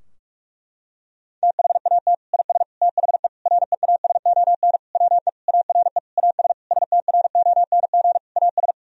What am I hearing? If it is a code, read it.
Morse code, heard as a keyed tone, about 46 wpm, 700 Hz. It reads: THAT IS THE REASON WE ARE AS STRONG AS
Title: Sentences from Top 1000 Words - Review Part 4 - 45wpm